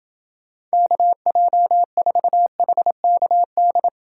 Morse code, heard as a keyed tone, about 29 wpm, 700 Hz. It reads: KJ4HKD